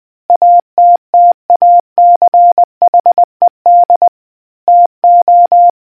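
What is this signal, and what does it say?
Morse code, heard as a keyed tone, about 20 wpm, 700 Hz. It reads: ATTACHED TO
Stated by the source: Morse quiz question for naval history buffs from the RNZN Communicators Association.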